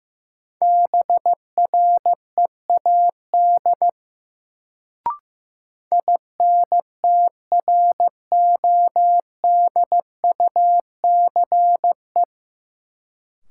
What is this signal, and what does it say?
Morse code, heard as a keyed tone, about 15 wpm, 700 Hz. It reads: BREAD INTRODUCE
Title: Sentences from Top 1600 Words - New Word Drill Alt - 15wpm